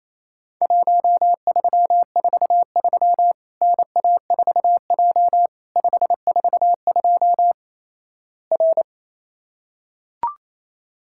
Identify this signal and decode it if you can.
Morse code, heard as a keyed tone, about 28 wpm, 700 Hz. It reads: 1343 NA4J 542 R
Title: Worked All Europe DX Contest - QTC Exchange - 28wpm